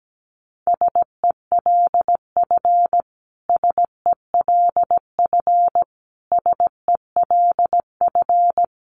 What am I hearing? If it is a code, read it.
Morse code, heard as a keyed tone, about 17 wpm, 700 Hz. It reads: SELF SELF SELF
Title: Top 200 Words Repeated 17wpm